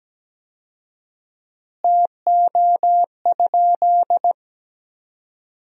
Morse code, heard as a keyed tone, about 17 wpm, 700 Hz. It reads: TO?
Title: 4x Word-Spacing - Sentences from Top 100 Words - 17wpm